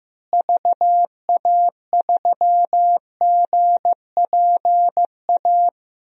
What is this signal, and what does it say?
Morse code, heard as a keyed tone, about 15 wpm, 700 Hz. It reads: VA3GPA
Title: DXCC-100 Call Signs - 6 Characters - Repeated 2x - 15wpm